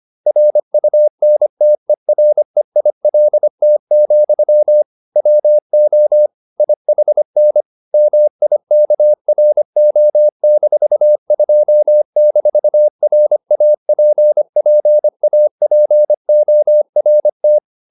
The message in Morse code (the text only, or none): RUNTEREILT, WO IHN MIKRO-2-RAPPAPORT